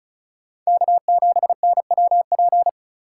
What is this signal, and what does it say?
Morse code, heard as a keyed tone, about 43 wpm, 700 Hz. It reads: K7NWP
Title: Speed Racing - USA 1x3 Call Signs 35wpm